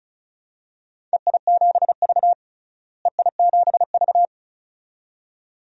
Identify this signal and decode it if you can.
Morse code, heard as a keyed tone, about 36 wpm, 700 Hz. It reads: EI7V EI7V
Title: DXCC-100 Call Signs - 4 Characters - Repeated 2x - 35wpm